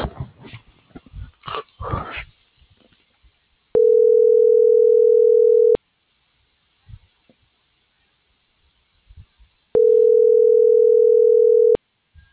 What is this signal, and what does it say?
Ringback tone (North America), 440 + 480 Hz.